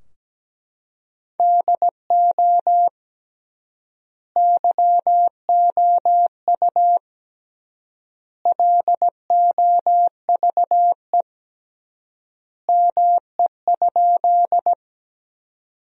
Morse code, heard as a keyed tone, about 17 wpm, 700 Hz. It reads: DO YOU LOVE ME?
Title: 3x Word-Spacing - Sentences from Top 400 Words - 17wpm